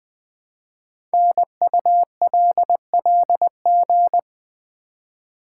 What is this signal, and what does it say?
Morse code, heard as a keyed tone, about 20 wpm, 700 Hz. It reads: NULLG
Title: Code Groups - 5 Letter:Numbers - 20wpm